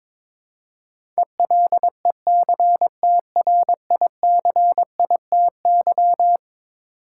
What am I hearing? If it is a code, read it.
Morse code, heard as a keyed tone, about 22 wpm, 700 Hz. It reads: ELECTRICITY
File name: practice / 22wpm / Eleven Letter Words - 22wpm